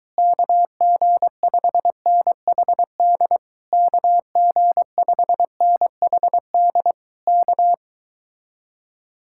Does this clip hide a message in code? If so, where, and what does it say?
Yes: Morse code from 0.2 to 7.7 s, heard as a keyed tone, about 23 wpm, 700 Hz. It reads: KG5NHD KG5NHD K